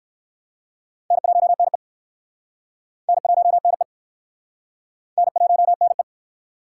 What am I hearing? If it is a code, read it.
Morse code, heard as a keyed tone, about 65 wpm, 700 Hz. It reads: N1NE N1NE N1NE